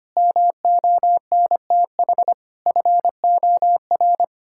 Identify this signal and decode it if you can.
Morse code, heard as a keyed tone, about 25 wpm, 700 Hz. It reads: MONTH FOR
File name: practice / 25wpm / Difficult Sentences with Hints - 25wpm